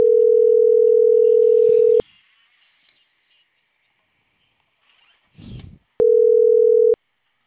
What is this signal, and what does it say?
Ringback tone (North America), 440 + 480 Hz.
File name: CENTRAL CAEA / out agcaea03